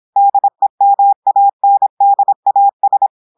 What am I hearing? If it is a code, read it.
Morse code, heard as a keyed tone, about 26 wpm, 800 Hz. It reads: DEMANDAS
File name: portugais texte en clair 2 vitesse 26 mots minutes